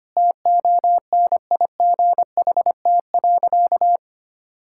Morse code, heard as a keyed tone, about 25 wpm, 700 Hz. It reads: TONIGHT.
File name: practice / 25wpm / Sentences with 3-Word Phrases - 25wpm